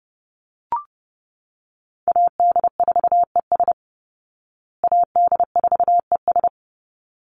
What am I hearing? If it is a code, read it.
Morse code, heard as a keyed tone, about 30 wpm, 700 Hz. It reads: AD4ES AD4ES
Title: Call Signs Repeated 3x - 30wpm